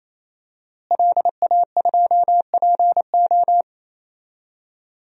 Morse code, heard as a keyed tone, about 27 wpm, 700 Hz. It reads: LA2PO
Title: DXCC-100 Call Signs - 5 Characters - Repeated 3x - 28wpm